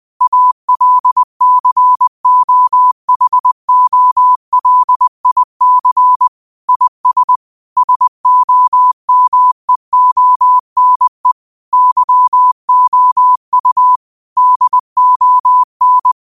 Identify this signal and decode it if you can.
Morse code, heard as a keyed tone, about 20 wpm, 1000 Hz. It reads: ALCOHOLIC IS SOMEONE YOU DON